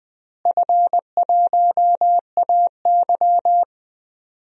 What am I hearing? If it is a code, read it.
Morse code, heard as a keyed tone, about 20 wpm, 700 Hz. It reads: F1AY
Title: DXCC-100 Call Signs - 4 Characters - Repeated 3x - 20wpm